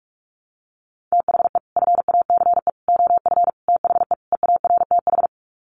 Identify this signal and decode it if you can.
Morse code, heard as a keyed tone, about 44 wpm, 700 Hz. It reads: THE FACE OF THE EARTH
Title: Sets of 5 Words Encore 45wpm